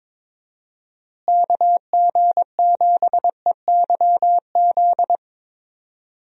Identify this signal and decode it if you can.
Morse code, heard as a keyed tone, about 22 wpm, 700 Hz. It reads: KG7EYZ
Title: USA Call Signs - 6 Characters - Repeated 3x - 22wpm